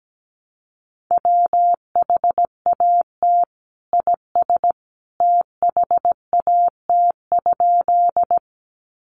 Morse code, heard as a keyed tone, about 17 wpm, 700 Hz. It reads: WHAT IS THAT?
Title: Sentences from Top 100 Words 17wpm